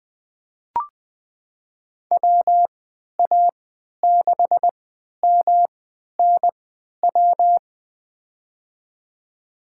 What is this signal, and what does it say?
Morse code, heard as a keyed tone, about 21 wpm, spaced as for 12 wpm, 700 Hz. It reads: WA6MNW
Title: USA Call Signs - 6 Characters - 3x Character Spacing - 20wpm